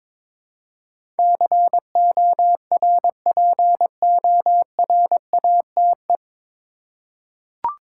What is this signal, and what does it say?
Morse code, heard as a keyed tone, about 23 wpm, 700 Hz. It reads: CORPORATE
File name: 9 Letter Words - First 3 Letters Given - 22wpm